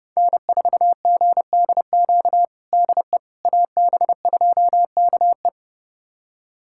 Morse code, heard as a keyed tone, about 32 wpm, 700 Hz. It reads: N4GDQ DE AB2KE
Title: USA Callsign DE Callsign - 5 Characters - 30wpm